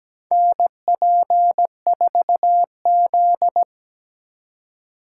Morse code, heard as a keyed tone, about 17 wpm, 700 Hz. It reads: NP4Z
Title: USA Call Signs - 4 Characters - 2x Character Spacing - 17wpm